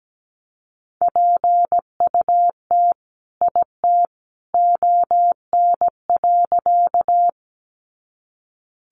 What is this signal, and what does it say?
Morse code, heard as a keyed tone, about 17 wpm, 700 Hz. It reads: PUT IT ON.
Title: Sentences from Top 200 Words 17wpm